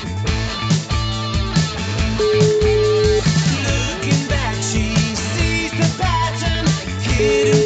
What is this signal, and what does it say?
Ringback tone (Europe, 425 Hz), with noise.